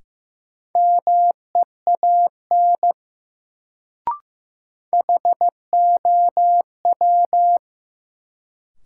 Morse code, heard as a keyed tone, about 15 wpm, 700 Hz. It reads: MEAN HOW
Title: Sets of 2 Words - New Word Drill - 15wpm